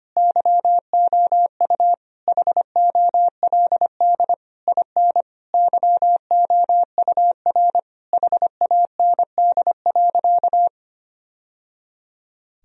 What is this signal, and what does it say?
Morse code, heard as a keyed tone, about 25 wpm, 700 Hz. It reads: YOU HOLD IN YOUR HAND.
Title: Sentences from Top 1000 Words - Review Part 2 - 25wpm